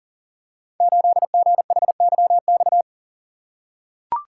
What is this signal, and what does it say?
Morse code, heard as a keyed tone, about 42 wpm, 700 Hz. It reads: 8GHYX
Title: Code Groups - 5 Letter:Numbers - 40wpm